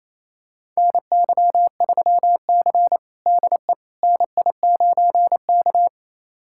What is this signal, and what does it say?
Morse code, heard as a keyed tone, about 28 wpm, 700 Hz. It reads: NY3C DE NI9K
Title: USA Callsign DE Callsign - 4 Characters - 28wpm